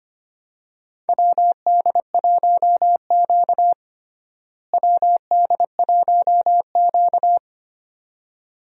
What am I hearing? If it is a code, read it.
Morse code, heard as a keyed tone, about 25 wpm, 700 Hz. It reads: WD1Q WD1Q
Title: USA Call Signs - 4 Characters - Repeated 2x - 25wpm